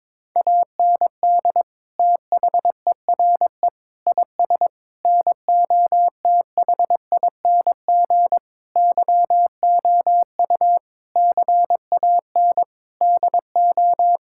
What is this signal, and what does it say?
Morse code, heard as a keyed tone, about 22 wpm, 700 Hz. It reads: AND THERE IS NOTHING YOU CAN DO
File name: practice / 22wpm / Sentences with 8-Word Phrases - 22wpm